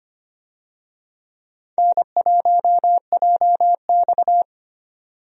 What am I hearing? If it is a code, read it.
Morse code, heard as a keyed tone, about 25 wpm, 700 Hz. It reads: N1JX
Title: USA Call Signs - 4 Characters - 25wpm